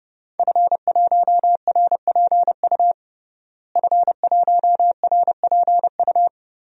Morse code, heard as a keyed tone, about 30 wpm, 700 Hz. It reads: F1RPU F1RPU
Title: DXCC-100 Call Signs - 5 Characters - Repeated 2x - 30wpm